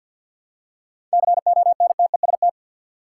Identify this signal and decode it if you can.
Morse code, heard as a keyed tone, about 53 wpm, 700 Hz. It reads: CONTEST
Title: Seven Letter Words - 50wpm